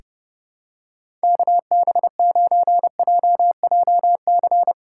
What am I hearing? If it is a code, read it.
Morse code, heard as a keyed tone, about 30 wpm, 700 Hz. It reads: KB9JJC
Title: USA Call Signs - 4x Character Spacing - 30wpm